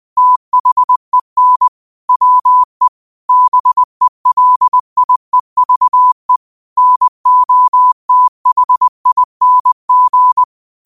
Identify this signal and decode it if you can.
Morse code, heard as a keyed tone, about 20 wpm, 1000 Hz. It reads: THEN WE BELIEVE NOTHING